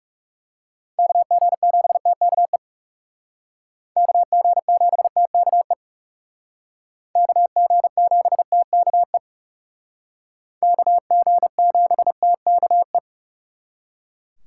Morse code, heard as a keyed tone, about 38 wpm, 700 Hz. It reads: KG7TKE KG7TKE KG7TKE KG7TKE